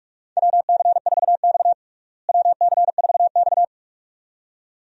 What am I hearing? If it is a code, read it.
Morse code, heard as a keyed tone, about 46 wpm, 700 Hz. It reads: WC4X WC4X
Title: USA Call Signs - 4 Characters - Repeated 2x - 45wpm